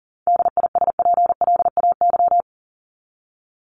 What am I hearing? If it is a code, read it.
Morse code, heard as a keyed tone, about 40 wpm, 700 Hz. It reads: DISPLAY
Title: Top 1001-1250 Words 40wpm